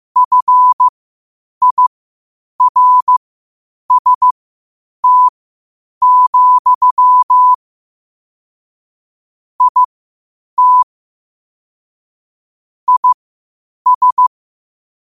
Morse code, heard as a keyed tone, about 15 wpm, spaced as for 8 wpm, 1000 Hz. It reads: FIRST, IT IS
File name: qotdfiles / 8wpm